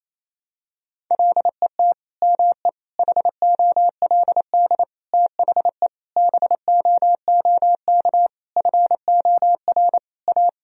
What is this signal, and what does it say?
Morse code, heard as a keyed tone, about 28 wpm, 700 Hz. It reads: LET ME HOLD THE BOOK FOR A
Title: Sentences from Top 1000 Words - Review Part 2 - 28wpm